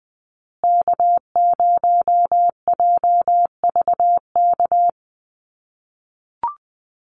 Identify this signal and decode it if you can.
Morse code, heard as a keyed tone, about 19 wpm, 700 Hz. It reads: K0JVK
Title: USA Call Signs Repeated 3x 20wpm